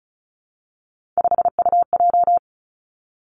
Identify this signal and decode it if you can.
Morse code, heard as a keyed tone, about 36 wpm, 700 Hz. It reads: 5UJ
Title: Three Challenging Letter-Number Combinations 35wpm